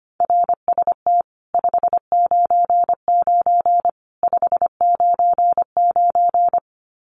Morse code, heard as a keyed tone, about 25 wpm, 700 Hz. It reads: RST 599 599
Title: Common QSO Phrases 25wpm